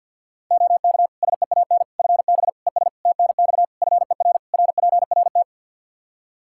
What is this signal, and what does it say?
Morse code, heard as a keyed tone, about 52 wpm, 700 Hz. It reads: OK SEAN FB ES TNX FER RPRT
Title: Practice QSO - Exchange 2 - Info - 50wpm